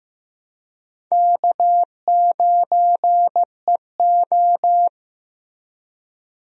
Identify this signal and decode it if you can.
Morse code, heard as a keyed tone, about 15 wpm, 700 Hz. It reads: K9EO
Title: USA Call Signs - 4 Characters - Repeated 3x - 15wpm